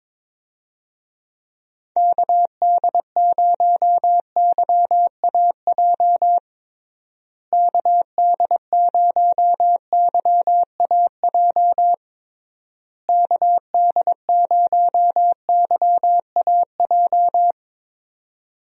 Morse code, heard as a keyed tone, about 22 wpm, 700 Hz. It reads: KD0YAJ KD0YAJ KD0YAJ
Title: USA Call Signs - 6 Characters - Repeated 3x - 22wpm